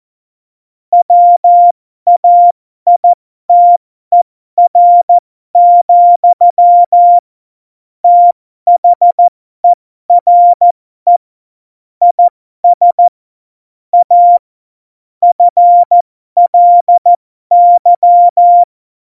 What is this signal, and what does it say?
Morse code, heard as a keyed tone, about 14 wpm, spaced as for 12 wpm, 700 Hz. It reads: WAITER, THERE IS A FLY
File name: Morsekurs / Witze